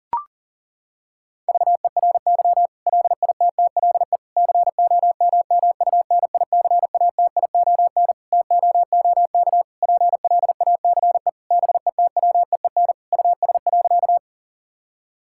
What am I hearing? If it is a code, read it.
Morse code, heard as a keyed tone, about 40 wpm, 700 Hz. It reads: VERY LITTLE COMMUNICATION TOOK PLACE BETWEEN US.
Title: Sentences from Top 1000 Words - Review Part 6 - 40wpm